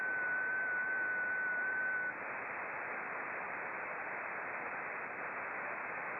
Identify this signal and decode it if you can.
FT8.
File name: contests / sept-vhf-k5tr- -0300